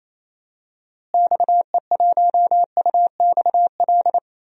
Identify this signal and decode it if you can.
Morse code, heard as a keyed tone, about 28 wpm, 700 Hz. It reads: XE1UXL
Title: DXCC-100 Call Signs - 6 Characters - Repeated 2x - 28wpm